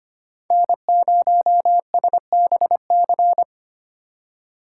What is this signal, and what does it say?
Morse code, heard as a keyed tone, about 25 wpm, 700 Hz. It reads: N0SBC
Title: USA Call Signs - 5 Characters - 3x Character Spacing - 25wpm